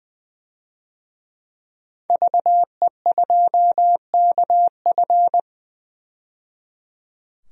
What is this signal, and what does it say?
Morse code, heard as a keyed tone, about 20 wpm, 700 Hz. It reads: VE2KF
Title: DXCC-100 Call Signs - 5 Characters - 20wpm